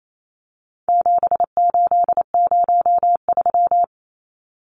Morse code, heard as a keyed tone, about 28 wpm, 700 Hz. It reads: 7803
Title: Numbers - 4-Digit 28wpm